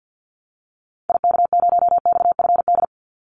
Morse code, heard as a keyed tone, about 49 wpm, 700 Hz. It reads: IK0XFD
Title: Call Signs Repeated 3x - 50wpm